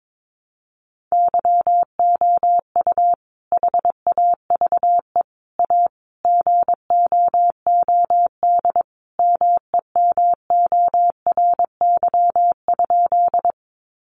Morse code, heard as a keyed tone, about 22 wpm, 700 Hz. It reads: YOU HAVE A GOOD MEMORY?
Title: Sentences from Top 1300 Words 22wpm